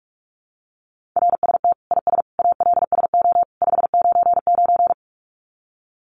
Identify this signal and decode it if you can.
Morse code, heard as a keyed tone, about 44 wpm, 700 Hz. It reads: RST IS ALSO 599
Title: Common QSO Phrases 45wpm